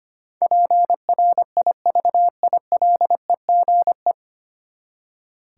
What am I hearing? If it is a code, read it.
Morse code, heard as a keyed tone, about 25 wpm, 700 Hz. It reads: PRIVILEGE
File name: Nine Letter Words - 25wpm